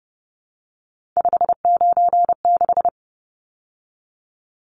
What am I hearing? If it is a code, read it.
Morse code, heard as a keyed tone, about 30 wpm, 700 Hz. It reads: 596
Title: Numbers - 3-Digit 30wpm